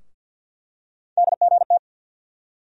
Morse code, heard as a keyed tone, about 52 wpm, 700 Hz. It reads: DGT
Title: USA Call Sign Postfixes - 3 Letters 50wpm